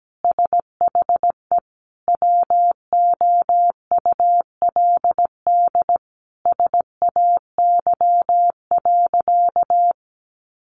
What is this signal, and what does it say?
Morse code, heard as a keyed tone, about 17 wpm, 700 Hz. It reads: SHE WOULD SAY.
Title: Sentences from Top 1500 Words 17wpm